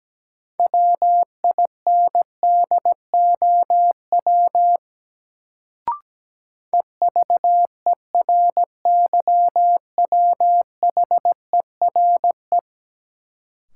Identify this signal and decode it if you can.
Morse code, heard as a keyed tone, about 17 wpm, 700 Hz. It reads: WINDOW EVERYWHERE
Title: Sentences from Top 1600 Words - New Word Drill - 17wpm